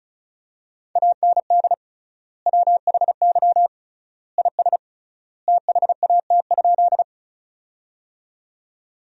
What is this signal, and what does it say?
Morse code, heard as a keyed tone, about 34 wpm, 700 Hz. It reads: AND WHY IS THAT?
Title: 3x Word-Spacing - Sentences from Top 200 Words - 35wpm